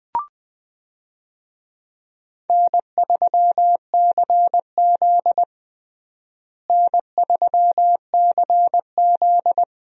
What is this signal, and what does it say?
Morse code, heard as a keyed tone, about 20 wpm, 700 Hz. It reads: N3CZ N3CZ
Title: USA Call Signs - 4 Characters - Repeated 2x - 20wpm